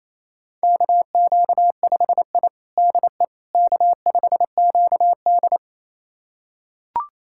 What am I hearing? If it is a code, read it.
Morse code, heard as a keyed tone, about 28 wpm, 700 Hz. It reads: KQ5I DE K5QD
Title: USA Callsign DE Callsign - 4 Characters - 28wpm